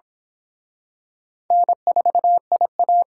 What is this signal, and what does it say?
Morse code, heard as a keyed tone, about 28 wpm, 700 Hz. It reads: N4IA